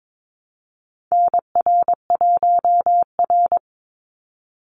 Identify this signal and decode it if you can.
Morse code, heard as a keyed tone, about 22 wpm, 700 Hz. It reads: NR1R
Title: Call Signs Repeated 2x - 22wpm